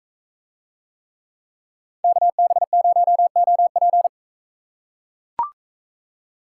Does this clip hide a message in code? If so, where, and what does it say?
Yes: Morse code from 2.0 to 4.1 s, heard as a keyed tone, about 39 wpm, 700 Hz. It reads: KB0OP